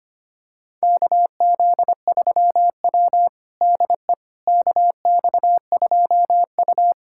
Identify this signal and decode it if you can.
Morse code, heard as a keyed tone, about 25 wpm, 700 Hz. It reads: KZ3W DE KX2U
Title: USA Callsign DE Callsign - 4 Characters - 25wpm